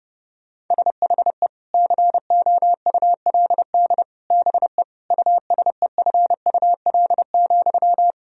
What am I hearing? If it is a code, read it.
Morse code, heard as a keyed tone, about 29 wpm, 700 Hz. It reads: SHE COULD BE USEFUL,
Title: Sentences from Top 1000 Words - Review Part 4 - 30wpm